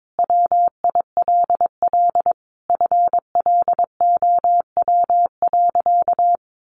Morse code, heard as a keyed tone, about 22 wpm, 700 Hz. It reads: WILL FLOW.
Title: Sentences from Top 1200 Words 22wpm